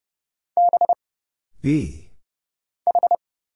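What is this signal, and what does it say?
Morse code, heard as a keyed tone, about 30 wpm, 700 Hz. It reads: B H
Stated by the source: Morse Code Ninja single-character practice (letters, numbers, punctuation, prosigns) at 30 wpm, by Kurt Zoglmann (AD0WE).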